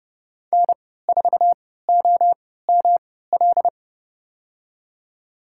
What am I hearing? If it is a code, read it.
Morse code, heard as a keyed tone, about 30 wpm, spaced as for 17 wpm, 700 Hz. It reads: N4OML
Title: USA Call Signs - 5 Characters - 3x Character Spacing - 30wpm